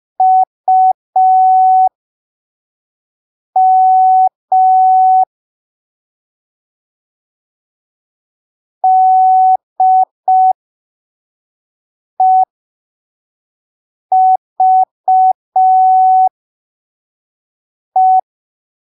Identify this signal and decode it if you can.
Morse code, heard as a keyed tone, about 5 wpm, spaced as for 3 wpm, 750 Hz. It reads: UM DEVE
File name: arquivos / texto / texto 20 05ppm